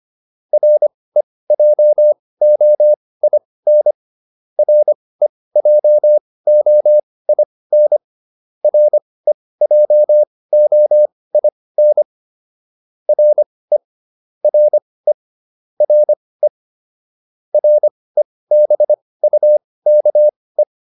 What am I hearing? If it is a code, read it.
Morse code, heard as a keyed tone, about 25 wpm, spaced as for 18 wpm, 600 Hz. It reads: REJOIN REJOIN REJOIN RE RE RE REBUKE